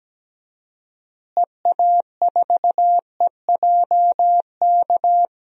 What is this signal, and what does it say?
Morse code, heard as a keyed tone, about 17 wpm, 700 Hz. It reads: EA4EJK
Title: DXCC-100 Call Signs - 6 Characters - Repeated 2x - 17wpm